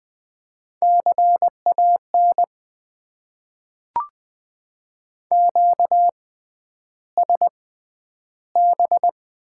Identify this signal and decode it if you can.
Morse code, heard as a keyed tone, about 20 wpm, spaced as for 10 wpm, 700 Hz. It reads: CAN QSB
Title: Top 90 QSO Elements - 6x Character Spacing - 20wpm